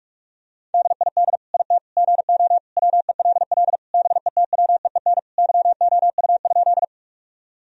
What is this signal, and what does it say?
Morse code, heard as a keyed tone, about 46 wpm, 700 Hz. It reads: DID IT GO WELL BETWEEN YOU?
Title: Sentences from Top 1000 Words - Review Part 1 - 45wpm